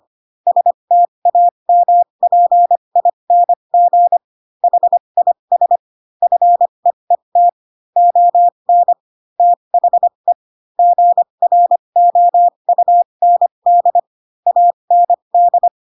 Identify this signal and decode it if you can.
Morse code, heard as a keyed tone, about 25 wpm, spaced as for 22 wpm, 700 Hz. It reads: STAMPING HIS FEET ON THE GROUND AND